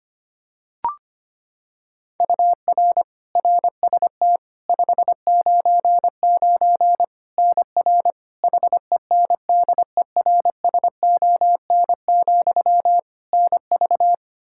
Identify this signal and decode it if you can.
Morse code, heard as a keyed tone, about 25 wpm, 700 Hz. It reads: UR RST 599 NR HENDERSON, NV
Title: Practice QSO - Exchange 1 - Beginning - 25wpm